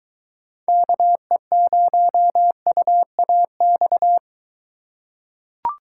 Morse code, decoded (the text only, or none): KE0UAX